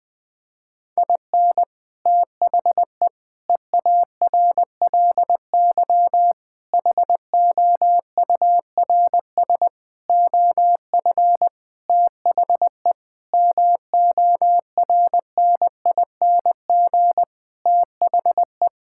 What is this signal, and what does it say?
Morse code, heard as a keyed tone, about 20 wpm, 700 Hz. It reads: IN THE EARLY HOURS OF THE MORNING THE
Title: Sentences with 7-Word Phrases - 20wpm